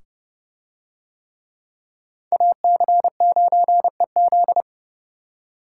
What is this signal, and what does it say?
Morse code, heard as a keyed tone, about 26 wpm, 700 Hz. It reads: AC9EZ